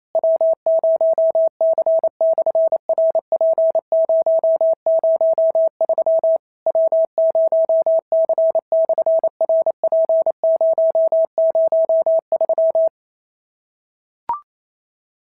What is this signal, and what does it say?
Morse code, heard as a keyed tone, about 28 wpm, 650 Hz. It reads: W0C/RP003 W0C/RP003
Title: SOTA QSO Practice 28wpm